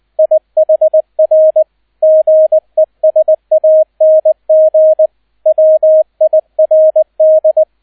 Morse code, heard as a keyed tone, about 20 wpm, 630 Hz, with radio noise. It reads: IHR GESANG WIRD
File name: Erster Streich 20wpm